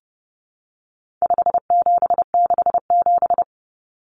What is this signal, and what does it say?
Morse code, heard as a keyed tone, about 30 wpm, 700 Hz. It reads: 5767
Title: Common Cut Numbers - 4-Digit 30wpm